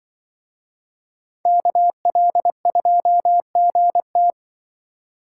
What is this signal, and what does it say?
Morse code, heard as a keyed tone, about 24 wpm, 700 Hz. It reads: KL2GT
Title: Speed Racing - USA Call Signs 20wpm